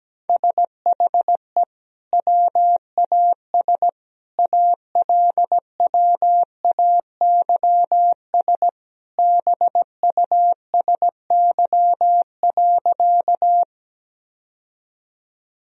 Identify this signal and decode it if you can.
Morse code, heard as a keyed tone, about 17 wpm, 700 Hz. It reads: SHE WAS ALWAYS BUSY.
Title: Sentences from Top 1000 Words - Review Part 3 - 17wpm